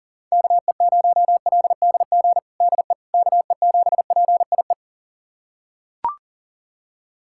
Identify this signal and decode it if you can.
Morse code, heard as a keyed tone, about 37 wpm, 700 Hz. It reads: KE0LDG DE KE7PIE
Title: USA Callsign DE Callsign - 6 Characters - 40wpm